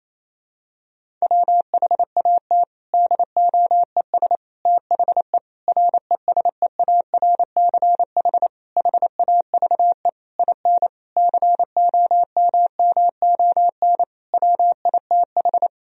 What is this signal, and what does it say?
Morse code, heard as a keyed tone, about 28 wpm, 700 Hz. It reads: WHAT DOES THE RESEARCH HAVE IN COMMON WITH
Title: Sentences from Top 1000 Words - Review Part 4 - 28wpm